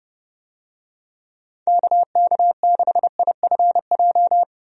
Morse code, heard as a keyed tone, about 30 wpm, 700 Hz. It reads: KK6IFJ
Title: USA Call Signs - 6 Characters - 30wpm